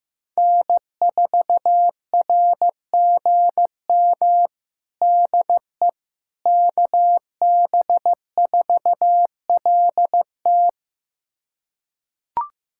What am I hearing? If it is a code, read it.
Morse code, heard as a keyed tone, about 15 wpm, 700 Hz. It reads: N4RGM DE KB4LT
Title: USA Callsign DE Callsign - 5 Characters - 15wpm